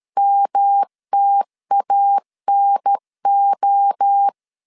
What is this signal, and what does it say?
Morse code, heard as a keyed tone, about 13 wpm, 800 Hz. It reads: MTANO
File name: CW / DET / 1 3 1500C S B 12wpm 2828